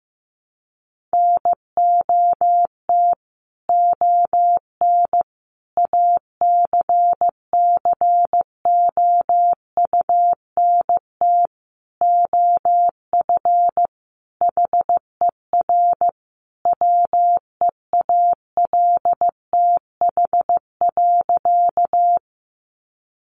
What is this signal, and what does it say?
Morse code, heard as a keyed tone, about 15 wpm, 700 Hz. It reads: NOT ON ACCOUNT OF HER WEALTH.
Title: Sentences from Top 1200 Words 15wpm